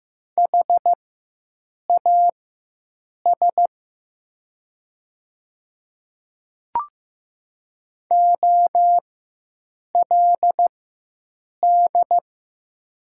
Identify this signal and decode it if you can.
Morse code, heard as a keyed tone, about 15 wpm, spaced as for 7 wpm, 700 Hz. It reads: HAS OLD
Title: 4x Character Spacing - Top 300 Words - 15wpm